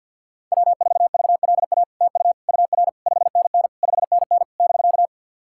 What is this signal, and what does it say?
Morse code, heard as a keyed tone, about 50 wpm, 700 Hz. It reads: W4VLA TU UR 5NN 5NN <BK>
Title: SOTA QSO Practice 50wpm